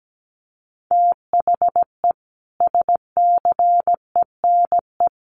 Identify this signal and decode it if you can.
Morse code, heard as a keyed tone, about 17 wpm, 700 Hz. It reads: THE SCENE
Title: Sentences from Top 1100 Words 17wpm